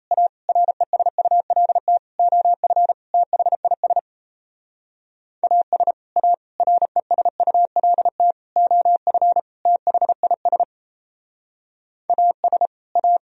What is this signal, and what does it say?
Morse code, heard as a keyed tone, about 36 wpm, 700 Hz. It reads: A RESULT OF THIS AS A RESULT OF THIS AS A